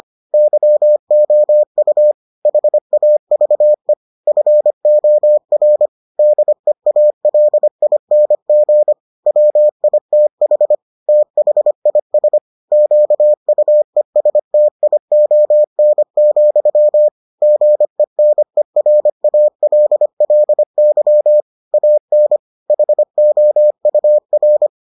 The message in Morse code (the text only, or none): YOU HAVE FOR DEALING WITH THIS QUESTION, GENERALLY AN HOUR